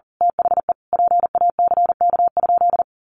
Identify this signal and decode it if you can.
Morse code, heard as a keyed tone, about 40 wpm, 700 Hz. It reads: THE PACK?
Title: Sentences from Top 1300 Words 40wpm